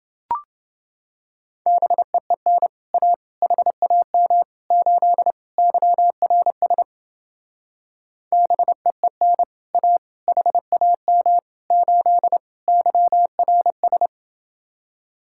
Morse code, heard as a keyed tone, about 29 wpm, 700 Hz. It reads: BEEN A HAM 8 YRS BEEN A HAM 8 YRS